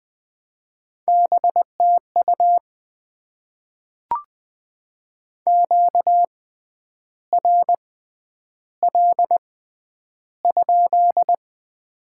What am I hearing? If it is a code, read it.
Morse code, heard as a keyed tone, about 20 wpm, spaced as for 9 wpm, 700 Hz. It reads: BTU QRL?
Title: Top 90 QSO Elements - 6x Character Spacing - 20wpm